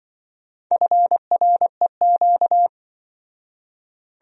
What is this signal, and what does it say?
Morse code, heard as a keyed tone, about 24 wpm, 700 Hz. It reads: FREQ
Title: Speed-Racing Abbreviations - 20wpm